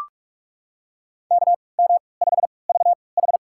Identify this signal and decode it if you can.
Morse code, heard as a keyed tone, about 45 wpm, spaced as for 25 wpm, 700 Hz. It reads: KM5VH
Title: USA Call Signs - 5 Characters - 3x Character Spacing - 45wpm